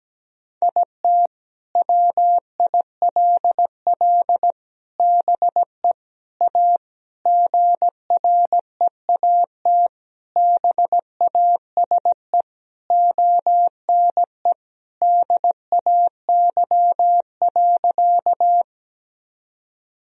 Morse code, heard as a keyed tone, about 17 wpm, 700 Hz. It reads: IT WILL BE A GREAT BASE ONE DAY.
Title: Sentences from Top 1000 Words - Review Part 1 - 17wpm